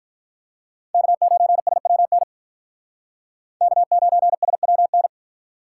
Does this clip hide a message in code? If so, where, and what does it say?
Yes: Morse code from 0.9 to 5.1 s, heard as a keyed tone, about 45 wpm, 700 Hz. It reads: K9SWN K9SWN